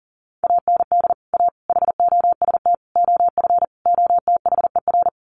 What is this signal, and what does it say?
Morse code, heard as a keyed tone, about 40 wpm, 700 Hz. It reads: AND A HOST OF OTHER
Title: Sets of 5 Words Encore2 40wpm